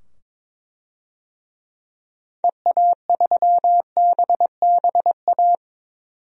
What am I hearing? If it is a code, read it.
Morse code, heard as a keyed tone, about 22 wpm, 700 Hz. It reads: EA3BBA
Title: DXCC-100 Call Signs - 6 Characters - 22wpm